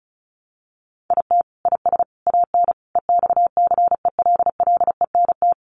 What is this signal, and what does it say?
Morse code, heard as a keyed tone, about 34 wpm, 700 Hz. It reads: IT IS AN EXCELLENT
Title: Sentences from Top 1000 Words 35wpm